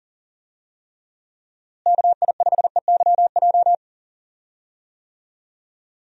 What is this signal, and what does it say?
Morse code, heard as a keyed tone, about 40 wpm, 700 Hz. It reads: KI5EYJ
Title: USA Call Signs - 6 Characters - Repeated 2x - 40wpm